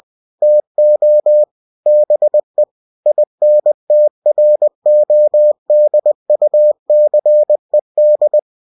Morse code, heard as a keyed tone, about 20 wpm, 600 Hz. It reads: TO BE INTRODUCED
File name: cw-books / winnie-the-pooh / winnie-20wpm-page-0000